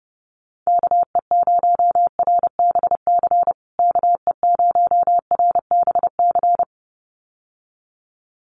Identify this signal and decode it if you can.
Morse code, heard as a keyed tone, about 30 wpm, 700 Hz. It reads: KE0RBC KE0RBC
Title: USA Call Signs Repeated 2x 30wpm